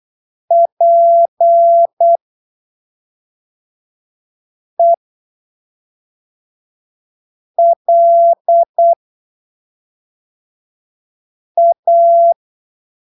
Morse code, heard as a keyed tone, about 8 wpm, spaced as for 3 wpm, 680 Hz. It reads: PELA
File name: arquivos / texto / texto 14 08ppm